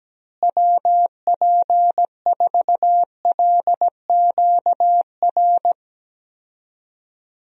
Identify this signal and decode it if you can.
Morse code, heard as a keyed tone, about 17 wpm, 700 Hz. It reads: WP4LQR
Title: USA Call Signs - 6 Characters - Repeated 3x - 17wpm